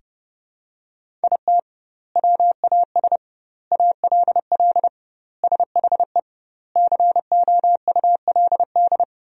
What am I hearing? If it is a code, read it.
Morse code, heard as a keyed tone, about 30 wpm, 700 Hz. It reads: IT WAS ALL SHE COULD
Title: 2x Word-Spacing - Sentences from Top 400 Words - 30wpm